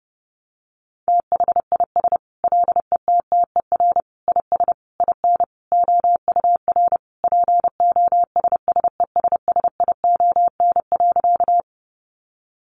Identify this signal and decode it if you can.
Morse code, heard as a keyed tone, about 28 wpm, 700 Hz. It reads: THIS LETTER IS IN OUR POSSESSION.
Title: Sentences from Top 1400 Words 30wpm